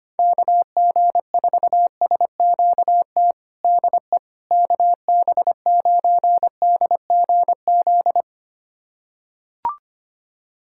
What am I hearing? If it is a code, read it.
Morse code, heard as a keyed tone, about 25 wpm, 700 Hz. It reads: KG4SQT DE KB9DGZ
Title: USA Callsign DE Callsign - 6 Characters - 25wpm